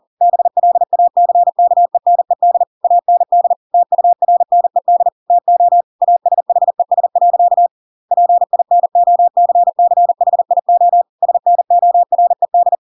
Morse code, heard as a keyed tone, about 40 wpm, 700 Hz. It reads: BLACKENED AND TURNED TO ASHES. PINOCCHIO SNORED
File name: Pinocchio / Pinocchio ch6 40 40 ik1hge